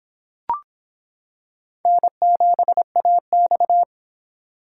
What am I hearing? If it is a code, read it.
Morse code, heard as a keyed tone, about 26 wpm, 700 Hz. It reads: N7AX